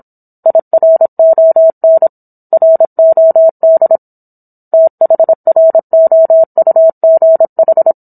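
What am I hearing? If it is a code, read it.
Morse code, heard as a keyed tone, about 27 wpm, 650 Hz. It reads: IRON ROD THROUGH
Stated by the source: Morse code practice of a book chapter at 26 WPM.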